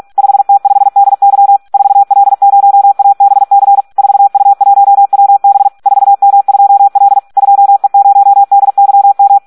Morse code, heard as a keyed tone, about 46 wpm, 800 Hz. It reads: 5T5DY 4L0ABC 4U1WB 4M2L 2E0DYM
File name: Audio practica CW / SD 1 46